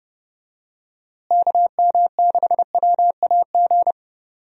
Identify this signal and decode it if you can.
Morse code, heard as a keyed tone, about 30 wpm, 700 Hz. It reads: KM6WAG
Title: USA Call Signs - 6 Characters - Repeated 3x - 30wpm